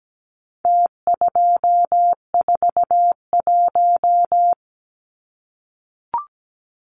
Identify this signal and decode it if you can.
Morse code, heard as a keyed tone, about 17 wpm, 700 Hz. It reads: T241
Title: Numbers - Contest Style - 4-Digit 17wpm